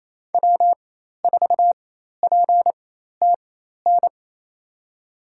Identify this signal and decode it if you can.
Morse code, heard as a keyed tone, about 28 wpm, spaced as for 13 wpm, 700 Hz. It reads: W4PTN
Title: USA Call Signs - 5 Characters - 4x Character Spacing - 28wpm